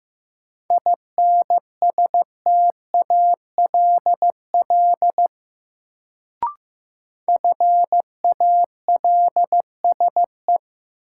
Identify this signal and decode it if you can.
Morse code, heard as a keyed tone, about 15 wpm, 700 Hz. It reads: INSTALL FALSE
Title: Sentences from Top 1600 Words - New Word Drill - 15wpm